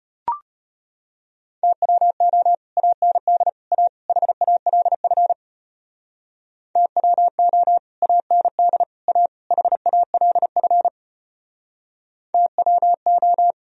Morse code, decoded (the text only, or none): TWO AND A HALF TWO AND A HALF TWO